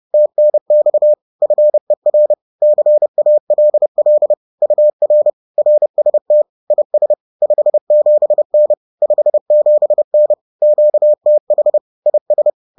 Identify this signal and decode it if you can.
Morse code, heard as a keyed tone, about 30 wpm, 600 Hz. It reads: TNX FER CALL UR RST IS 57N 57N QTH IS